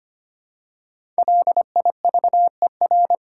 Morse code, heard as a keyed tone, about 25 wpm, 700 Hz. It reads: LIVER
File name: 5 Letter Words - First 3 Letters Given - 25wpm